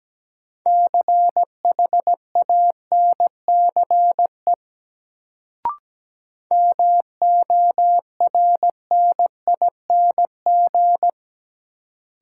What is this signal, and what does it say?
Morse code, heard as a keyed tone, about 17 wpm, 700 Hz. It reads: CHANCE MORNING
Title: Sets of 3 Words - New Word Drill - 17wpm